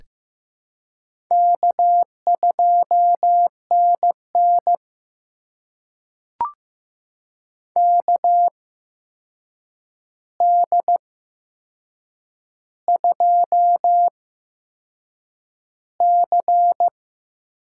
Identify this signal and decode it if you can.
Morse code, heard as a keyed tone, about 15 wpm, spaced as for 6 wpm, 700 Hz. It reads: K2NN KD2C
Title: USA Call Signs - 4 Characters - 8x Character Spacing - 15wpm